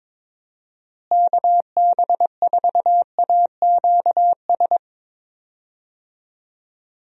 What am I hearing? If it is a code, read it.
Morse code, heard as a keyed tone, about 22 wpm, 700 Hz. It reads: KB4AQS
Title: USA Call Signs - 6 Characters - Repeated 2x - 22wpm